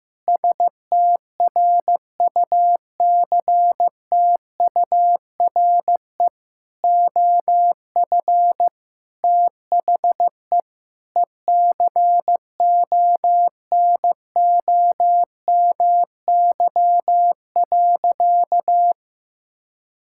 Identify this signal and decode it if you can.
Morse code, heard as a keyed tone, about 15 wpm, 700 Hz. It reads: STRUCTURE OF THE ECONOMY.
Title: Sentences from Top 1000 Words - Review Part 6 - 15wpm